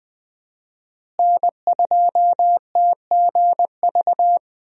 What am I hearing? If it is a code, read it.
Morse code, heard as a keyed tone, about 21 wpm, 700 Hz. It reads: N2TGV
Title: USA Call Signs - 5 Characters - Repeated 3x - 20wpm